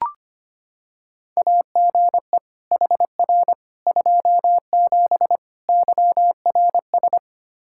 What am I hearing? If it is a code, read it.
Morse code, heard as a keyed tone, about 25 wpm, 700 Hz. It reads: AGE HR 27 YRS
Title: Practice QSO - Exchange 3 - More Info - 25wpm